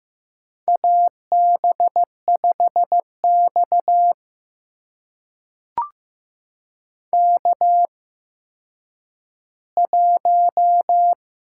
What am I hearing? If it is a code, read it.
Morse code, heard as a keyed tone, about 15 wpm, spaced as for 9 wpm, 700 Hz. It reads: AB5X K1
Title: USA Call Signs - 4 Characters - 8x Character Spacing - 15wpm